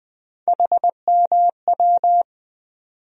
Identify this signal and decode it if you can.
Morse code, heard as a keyed tone, about 20 wpm, 700 Hz. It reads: HMW
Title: USA Call Sign Postfixes - 3 Letters 20wpm